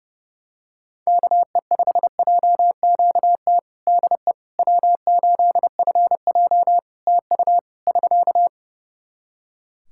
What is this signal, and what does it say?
Morse code, heard as a keyed tone, about 32 wpm, 700 Hz. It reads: KE5JQT DE W8FJ TU <SK>
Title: Practice QSO - Exchange 4 - Ending - 30wpm